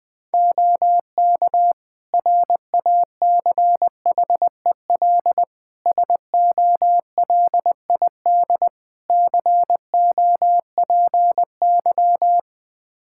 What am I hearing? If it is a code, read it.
Morse code, heard as a keyed tone, about 20 wpm, 700 Hz. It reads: OK RACHEL SOLID COPY
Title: K1USN SST QSO Practice 20wpm